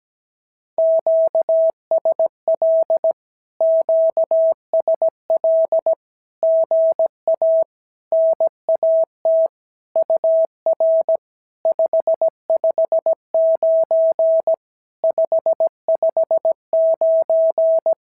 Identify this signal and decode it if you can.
Morse code, heard as a keyed tone, about 17 wpm, 650 Hz. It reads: QSL QSL GA NAT UR 559 559
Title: SOTA QSO Practice 17wpm